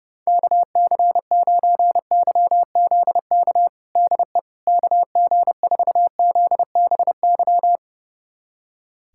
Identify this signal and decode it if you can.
Morse code, heard as a keyed tone, about 30 wpm, 700 Hz. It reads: KC9YZK DE KG4ZBY
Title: USA Callsign DE Callsign 30wpm